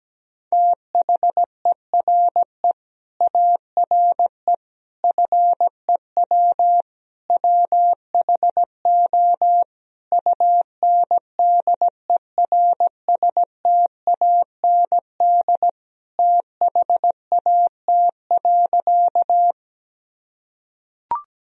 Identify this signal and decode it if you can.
Morse code, heard as a keyed tone, about 17 wpm, 700 Hz. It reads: THERE ARE FEW WHO UNDERSTAND THAT.
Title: Sentences from Top 1000 Words - Review Part 1 - 17wpm